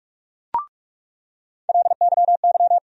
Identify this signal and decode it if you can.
Morse code, heard as a keyed tone, about 46 wpm, 700 Hz. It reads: LYY
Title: USA Call Sign Postfixes - 3 Letters 45wpm